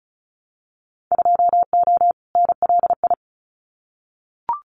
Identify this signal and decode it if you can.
Morse code, heard as a keyed tone, about 34 wpm, 700 Hz. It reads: 2O NLI
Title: Winter Field Day Exchanges 35wpm